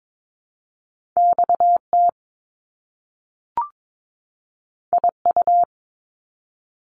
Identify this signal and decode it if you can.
Morse code, heard as a keyed tone, about 22 wpm, 700 Hz. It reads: XT IU
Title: Top Two Letter Combinations in Words 22wpm